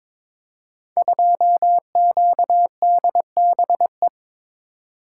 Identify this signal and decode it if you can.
Morse code, heard as a keyed tone, about 22 wpm, 700 Hz. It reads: 2QDBE
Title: Code Groups - 5 Letter:Numbers - 22wpm